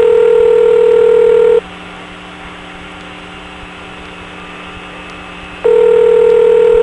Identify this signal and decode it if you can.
Ringback tone (North America), 440 + 480 Hz.